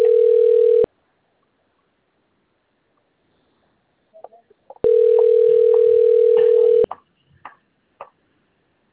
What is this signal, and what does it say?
Ringback tone (North America), 440 + 480 Hz.